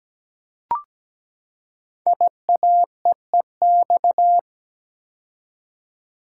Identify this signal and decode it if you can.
Morse code, heard as a keyed tone, about 17 wpm, 700 Hz. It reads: IAEEX
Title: Code Groups - 5 Letter:Numbers - 17wpm